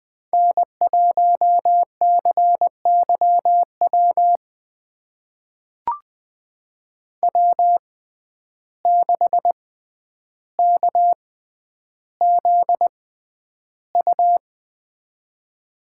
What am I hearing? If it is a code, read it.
Morse code, heard as a keyed tone, about 20 wpm, spaced as for 10 wpm, 700 Hz. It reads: N1CYW W6KZU
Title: USA Call Signs - 5 Characters - 6x Character Spacing - 20wpm